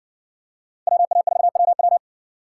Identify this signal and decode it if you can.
Morse code, heard as a keyed tone, about 60 wpm, 700 Hz. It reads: WA4WW